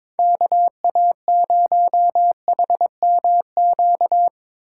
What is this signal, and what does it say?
Morse code, heard as a keyed tone, about 25 wpm, 700 Hz. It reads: KA0HMQ
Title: Speed Racing - USA 2x3 Call Signs 22wpm